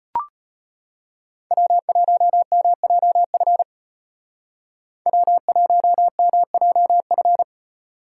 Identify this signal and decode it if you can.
Morse code, heard as a keyed tone, about 34 wpm, 700 Hz. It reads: W1MJF W1MJF